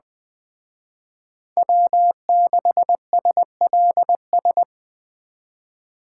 Morse code, heard as a keyed tone, about 20 wpm, 700 Hz. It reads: W6SLS